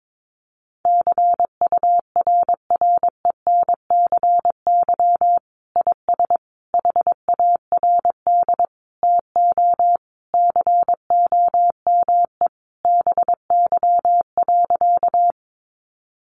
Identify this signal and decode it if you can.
Morse code, heard as a keyed tone, about 22 wpm, 700 Hz. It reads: CURRENCY IS HARD TO COME BY.
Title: Sentences from Top 1500 Words 22wpm